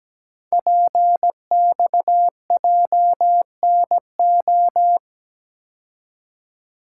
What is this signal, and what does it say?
Morse code, heard as a keyed tone, about 17 wpm, 700 Hz. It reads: PXJNO
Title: Code Groups - 5 Letters - 17wpm